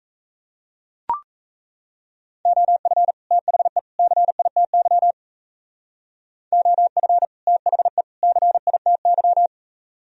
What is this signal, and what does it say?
Morse code, heard as a keyed tone, about 35 wpm, 700 Hz. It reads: OF THE CITY OF THE CITY
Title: Speed-Racing - Sets of 3 Words 28wpm